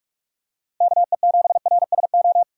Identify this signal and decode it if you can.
Morse code, heard as a keyed tone, about 46 wpm, 700 Hz. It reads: KE7RSO
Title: USA Call Signs - 6 Characters - Repeated 3x - 45wpm